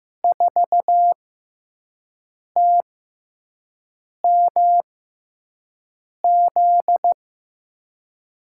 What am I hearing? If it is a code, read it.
Morse code, heard as a keyed tone, about 15 wpm, spaced as for 5 wpm, 700 Hz. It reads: 4TMZ
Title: USA Call Signs - 6 Characters - 6x Character Spacing - 15wpm